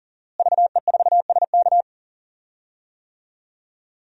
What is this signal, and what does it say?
Morse code, heard as a keyed tone, about 40 wpm, 700 Hz. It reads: VE4SK
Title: DXCC-100 Call Signs - 5 Characters - Repeated 2x - 40wpm